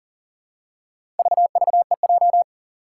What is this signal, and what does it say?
Morse code, heard as a keyed tone, about 40 wpm, 700 Hz. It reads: VVEJ
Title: Four Challenging Letter-Number Combinations 40wpm